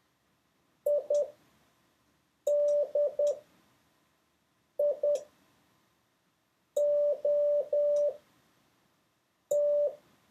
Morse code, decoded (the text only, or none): IDIOT